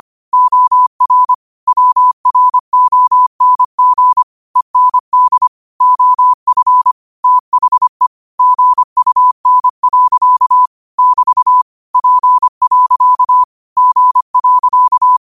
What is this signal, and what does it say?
Morse code, heard as a keyed tone, about 25 wpm, 1000 Hz. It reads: OR WRONG END OF THE GUN. <BT> P. G.
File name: qotdfiles / 25wpm